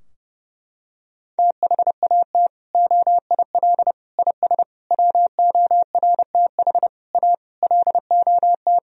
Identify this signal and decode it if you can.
Morse code, heard as a keyed tone, about 30 wpm, 700 Hz. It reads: THAT OIL IS WORTH A LOT
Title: Sentences from Top 1000 Words - Review Part 4 - 30wpm